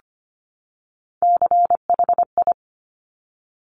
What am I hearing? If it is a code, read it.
Morse code, heard as a keyed tone, about 25 wpm, 700 Hz. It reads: CHI
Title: Top Three Letter Combinations in Words 25wpm